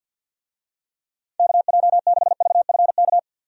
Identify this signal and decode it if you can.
Morse code, heard as a keyed tone, about 51 wpm, 700 Hz. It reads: KJ6VFK
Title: USA Call Signs - 6 Characters - 50wpm